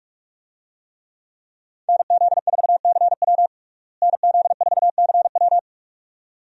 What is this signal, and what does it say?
Morse code, heard as a keyed tone, about 46 wpm, 700 Hz. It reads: NZ4CW NZ4CW
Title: USA Call Signs - 5 Characters - Repeated 2x - 45wpm